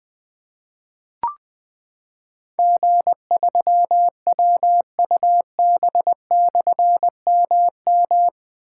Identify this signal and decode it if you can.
Morse code, heard as a keyed tone, about 20 wpm, 700 Hz. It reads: G3WUB/MM
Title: Difficult Call Signs 20wpm